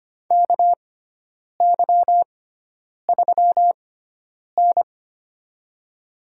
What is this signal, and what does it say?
Morse code, heard as a keyed tone, about 25 wpm, spaced as for 9 wpm, 700 Hz. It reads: KY3N
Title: USA Call Signs - 6x Character Spacing - 25wpm